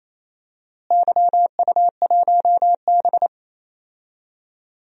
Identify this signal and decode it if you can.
Morse code, heard as a keyed tone, about 28 wpm, 700 Hz. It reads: YU1B
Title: DXCC-100 Call Signs - 4 Characters - Repeated 3x - 28wpm